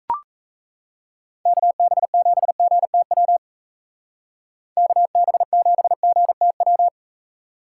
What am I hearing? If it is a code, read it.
Morse code, heard as a keyed tone, about 33 wpm, 700 Hz. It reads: KB7GTW KB7GTW
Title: Speed Racing - USA 2x3 Call Signs 28wpm